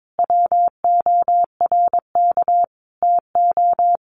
Morse code, heard as a keyed tone, about 22 wpm, 700 Hz. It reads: WORK TO
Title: Sets of 5 Words Encore2 22wpm